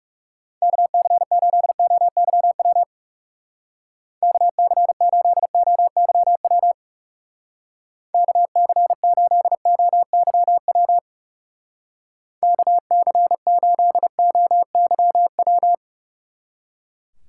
Morse code, heard as a keyed tone, about 36 wpm, 700 Hz. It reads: KC8OYW KC8OYW KC8OYW KC8OYW
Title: Speed Racing - USA 2x3 Call Signs 30wpm